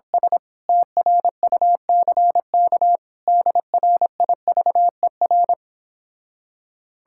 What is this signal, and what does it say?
Morse code, heard as a keyed tone, about 26 wpm, 700 Hz. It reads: S TRUCK DRIVER